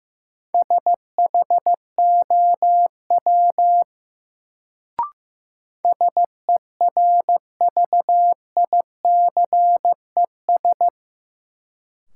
Morse code, heard as a keyed tone, about 15 wpm, 700 Hz. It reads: SHOW SERVICES
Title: Sets of 5 Words - New Word Drill - 15wpm